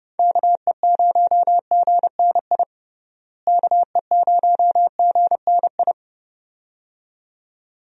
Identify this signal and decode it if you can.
Morse code, heard as a keyed tone, about 30 wpm, 700 Hz. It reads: KE0GNI KE0GNI
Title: USA Call Signs - 6 Characters - Repeated 2x - 30wpm